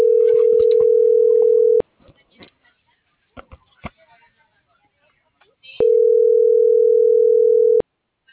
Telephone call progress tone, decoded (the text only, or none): ringback tone